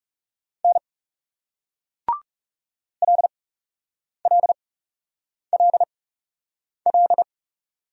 Morse code, heard as a keyed tone, about 39 wpm, spaced as for 8 wpm, 700 Hz. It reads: N LLLL